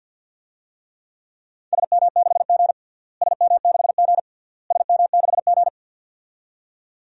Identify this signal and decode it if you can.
Morse code, heard as a keyed tone, about 51 wpm, 700 Hz. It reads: SM6G SM6G SM6G
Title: DXCC-100 Call Signs - 4 Characters - Repeated 3x - 50wpm